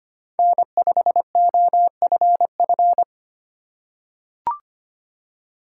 Morse code, heard as a keyed tone, about 25 wpm, 700 Hz. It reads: N5OFF
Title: USA Call Signs - 5 Characters - 3x Character Spacing - 25wpm